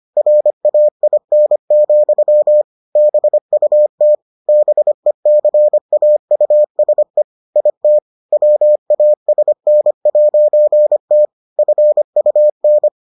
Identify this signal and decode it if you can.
Morse code, heard as a keyed tone, about 25 wpm, 600 Hz. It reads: RAIN, BUT BECAUSE IT WASN'T FUN